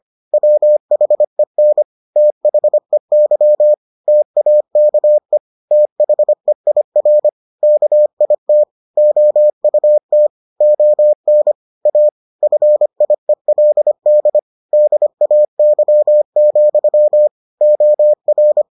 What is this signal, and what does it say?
Morse code, heard as a keyed tone, about 25 wpm, 600 Hz. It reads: WHEN THEY TAKE THEIR KIT OUT ON A FIELD DAY, OR